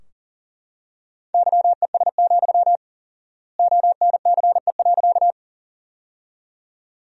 Morse code, heard as a keyed tone, about 40 wpm, 700 Hz. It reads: YES, ONCE.
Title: 4x Word-Spacing - Sentences from Top 500 Words - 40wpm